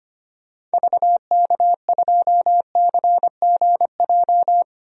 Morse code, heard as a keyed tone, about 25 wpm, 700 Hz. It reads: VK2CGJ
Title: DXCC-100 Call Signs - 6 Characters - Repeated 3x - 25wpm